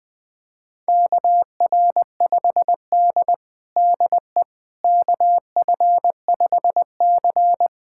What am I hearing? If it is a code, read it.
Morse code, heard as a keyed tone, about 21 wpm, 700 Hz. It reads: KR5D DE KF5C
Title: USA Callsign DE Callsign - 4 Characters - 20wpm